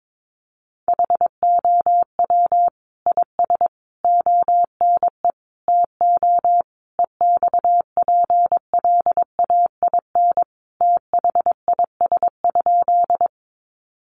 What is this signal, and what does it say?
Morse code, heard as a keyed tone, about 22 wpm, 700 Hz. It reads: HOW IS ONE TO EXPLAIN THIS?
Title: Sentences from Top 1100 Words 22wpm